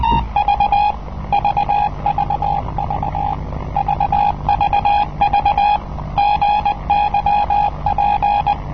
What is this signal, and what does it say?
Morse code, heard as a keyed tone, about 19 wpm, 820 Hz. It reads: V VVV VVV GYP
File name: rnws hongkong 'gyp' cw (1973)